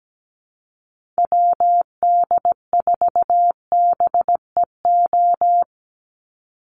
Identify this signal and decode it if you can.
Morse code, heard as a keyed tone, about 17 wpm, 700 Hz. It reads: WD4BEO
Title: USA Call Signs 17wpm